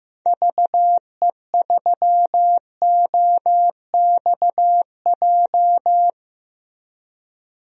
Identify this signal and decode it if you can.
Morse code, heard as a keyed tone, about 16 wpm, 700 Hz. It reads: VE3OXJ
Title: DXCC-100 Call Signs - 6 Characters - Repeated 3x - 15wpm